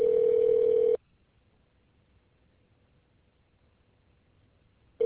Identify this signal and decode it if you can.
Ringback tone (North America), 440 + 480 Hz.